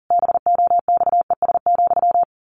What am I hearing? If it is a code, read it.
Morse code, heard as a keyed tone, about 40 wpm, 700 Hz. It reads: BOXES,
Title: A Sound Of Thunder 40wpm